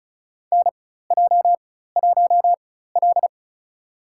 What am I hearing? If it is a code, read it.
Morse code, heard as a keyed tone, about 36 wpm, spaced as for 16 wpm, 700 Hz. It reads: NJ1L
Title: USA Call Signs - 4 Characters - 4x Character Spacing - 35wpm